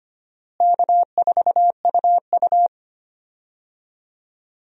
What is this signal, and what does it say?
Morse code, heard as a keyed tone, about 25 wpm, 700 Hz. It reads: K4UU
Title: USA Call Signs - 4 Characters - Repeated 2x - 25wpm